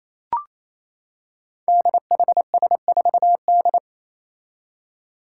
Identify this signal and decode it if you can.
Morse code, heard as a keyed tone, about 27 wpm, 700 Hz. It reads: DHS4D
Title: Five Challenging Letter-Number Combinations 28wpm